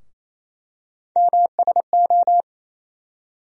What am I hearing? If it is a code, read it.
Morse code, heard as a keyed tone, about 28 wpm, 700 Hz. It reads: MSO
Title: USA Call Sign Postfixes - 3 Letters 28wpm